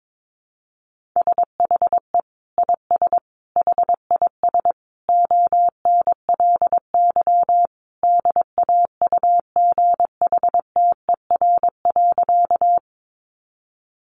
Morse code, heard as a keyed tone, about 22 wpm, 700 Hz. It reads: SHE IS HIS ONLY DAUGHTER.
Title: Sentences from Top 1500 Words 22wpm